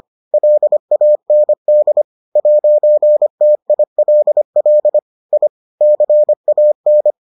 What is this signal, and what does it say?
Morse code, heard as a keyed tone, about 25 wpm, 600 Hz. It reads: LAND 'TILL I CAN